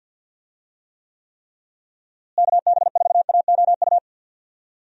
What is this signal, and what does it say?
Morse code, heard as a keyed tone, about 52 wpm, 700 Hz. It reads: KB4AOU
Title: USA Call Signs - 6 Characters - Repeated 3x - 50wpm